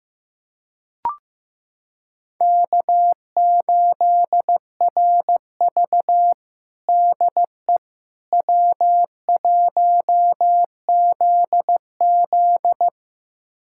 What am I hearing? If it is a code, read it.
Morse code, heard as a keyed tone, about 15 wpm, 700 Hz. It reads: K8RV DE W1ZZ
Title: USA Callsign DE Callsign - 4 Characters - 15wpm